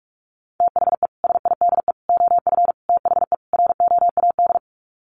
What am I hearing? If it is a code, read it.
Morse code, heard as a keyed tone, about 44 wpm, 700 Hz. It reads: THE SIDE OF THE ROAD
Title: Sets of 5 Words 45wpm